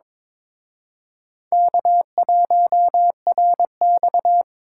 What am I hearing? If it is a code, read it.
Morse code, heard as a keyed tone, about 26 wpm, 700 Hz. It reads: K1RX
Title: Speed Racing - USA 1x2 Call Signs 22wpm